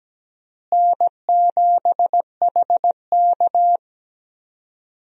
Morse code, heard as a keyed tone, about 17 wpm, 700 Hz. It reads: N7HK
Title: USA Call Signs - 4 Characters - Repeated 2x - 17wpm